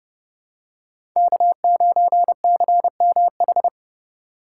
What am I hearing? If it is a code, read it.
Morse code, heard as a keyed tone, about 30 wpm, 700 Hz. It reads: K9CMH
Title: USA Call Signs - 5 Characters - 2x Character Spacing - 30wpm